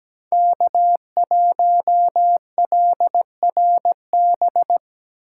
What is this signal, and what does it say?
Morse code, heard as a keyed tone, about 17 wpm, 700 Hz. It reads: K1LRB
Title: USA Call Signs - 5 Characters - 8x Character Spacing - 17wpm